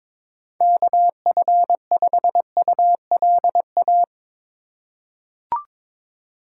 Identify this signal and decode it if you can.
Morse code, heard as a keyed tone, about 22 wpm, 700 Hz. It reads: KF5ULA
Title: USA Call Signs - 6 Characters - 2x Character Spacing - 22wpm